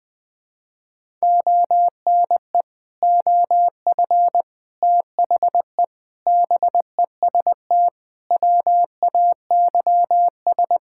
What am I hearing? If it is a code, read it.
Morse code, heard as a keyed tone, about 20 wpm, 700 Hz. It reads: ONE OF THE BEST WAYS
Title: Sentences with 6-Word Phrases - 20wpm